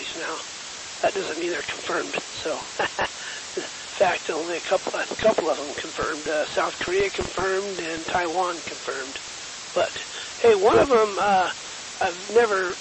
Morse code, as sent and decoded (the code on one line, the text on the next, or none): none